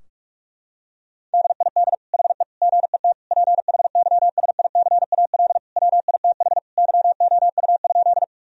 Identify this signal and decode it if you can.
Morse code, heard as a keyed tone, about 42 wpm, 700 Hz. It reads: DID HE GET PHYSICAL WITH YOU?
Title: Sentences from Top 1000 Words - Review Part 4 - 45wpm